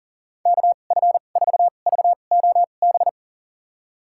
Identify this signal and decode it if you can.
Morse code, heard as a keyed tone, about 38 wpm, spaced as for 28 wpm, 700 Hz. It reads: KF4VOB
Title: USA Call Signs - 6 Characters - 2x Character Spacing - 40wpm